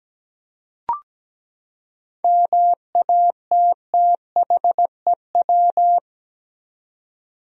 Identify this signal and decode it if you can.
Morse code, heard as a keyed tone, about 17 wpm, 700 Hz. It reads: MATTHEW
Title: USA Common Names - 17wpm